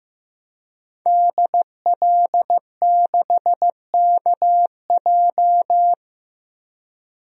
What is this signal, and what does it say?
Morse code, heard as a keyed tone, about 15 wpm, 700 Hz. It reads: DL6KJ
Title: DXCC-100 Call Signs - 5 Characters - 15wpm